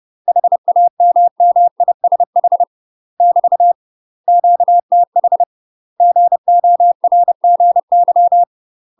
Morse code, heard as a keyed tone, about 30 wpm, 700 Hz. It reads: HAMMISH <BT> QTH GORGY
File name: UK MASTER QSOs TRAINING/30WPM